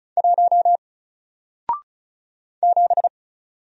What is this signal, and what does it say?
Morse code, heard as a keyed tone, about 36 wpm, 700 Hz. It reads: 1 7
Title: Single Character (Letter:Number:Punctuation:Prosign) Alt 35wpm